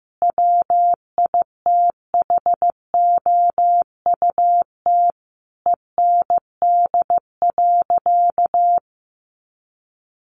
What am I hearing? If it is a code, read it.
Morse code, heard as a keyed tone, about 15 wpm, 700 Hz. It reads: WITHOUT END.
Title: Sentences from Top 600 Words 15wpm